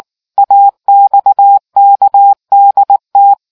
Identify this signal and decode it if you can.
Morse code, heard as a keyed tone, about 19 wpm, 800 Hz. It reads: AXKDT